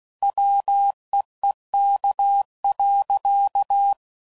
Morse code, heard as a keyed tone, about 16 wpm, 800 Hz. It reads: WEEK.
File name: CW Code Groups @15WPM / QSOs / WD0FBT